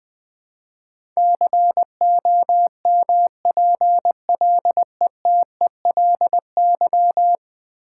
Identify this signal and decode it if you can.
Morse code, heard as a keyed tone, about 21 wpm, 700 Hz. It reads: COMPLETELY
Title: Ten Letter Words - 20wpm